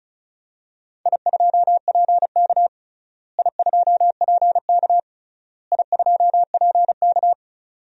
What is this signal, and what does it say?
Morse code, heard as a keyed tone, about 36 wpm, 700 Hz. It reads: I2PK I2PK I2PK
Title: DXCC-100 Call Signs - 4 Characters - Repeated 3x - 35wpm